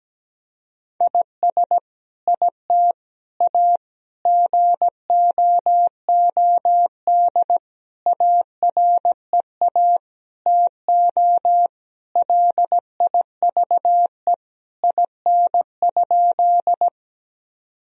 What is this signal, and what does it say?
Morse code, heard as a keyed tone, about 17 wpm, 700 Hz. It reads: IS IT A GOOD AREA TO LIVE IN?
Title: Sentences from Top 1000 Words - Review Part 2 - 17wpm